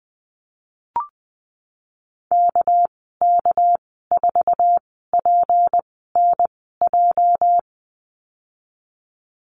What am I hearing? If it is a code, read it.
Morse code, heard as a keyed tone, about 20 wpm, spaced as for 15 wpm, 700 Hz. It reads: KK4PNJ
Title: 219 Lesson 35g - Number 0 - Call Signs 20wpm